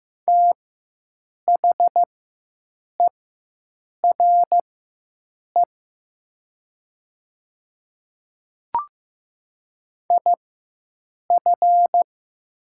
Morse code, heard as a keyed tone, about 15 wpm, spaced as for 7 wpm, 700 Hz. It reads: THERE IF
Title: 4x Character Spacing - Top 100 Words - 15wpm